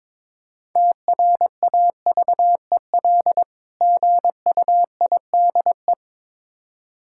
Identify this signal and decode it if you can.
Morse code, heard as a keyed tone, about 22 wpm, 700 Hz. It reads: TRAVEL GUIDE
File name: Difficult Words with Hints - 22wpm